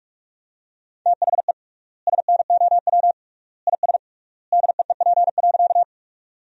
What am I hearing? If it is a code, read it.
Morse code, heard as a keyed tone, about 46 wpm, 700 Hz. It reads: THE SNOW IS DEEP.
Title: 3x Word-Spacing - Sentences from Top 500 Words - 45wpm